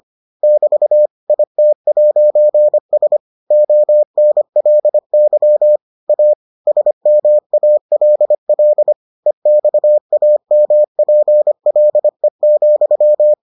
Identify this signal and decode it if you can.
Morse code, heard as a keyed tone, about 25 wpm, 600 Hz. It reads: <BT> IT'S ONLY A SMALL EXAMPLE,